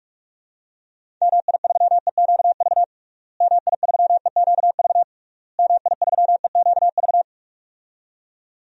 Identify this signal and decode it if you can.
Morse code, heard as a keyed tone, about 46 wpm, 700 Hz. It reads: MI3EQV MI3EQV MI3EQV
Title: DXCC-100 Call Signs - 6 Characters - Repeated 3x - 45wpm